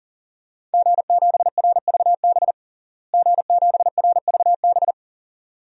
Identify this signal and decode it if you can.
Morse code, heard as a keyed tone, about 40 wpm, 700 Hz. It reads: G7RVB G7RVB
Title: DXCC-100 Call Signs - 5 Characters - Repeated 2x - 40wpm